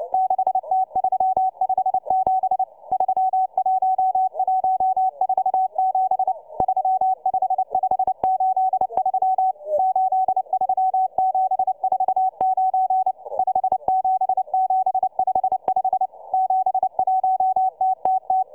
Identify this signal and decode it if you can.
Morse code, heard as a keyed tone, about 29 wpm, 750 Hz, with radio noise. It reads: 6T357 31147 35583 83749 57755 71TTT